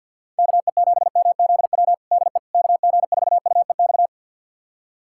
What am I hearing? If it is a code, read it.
Morse code, heard as a keyed tone, about 47 wpm, 700 Hz. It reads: KE7MZW DE KG4UEX
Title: USA Callsign DE Callsign - 6 Characters - 50wpm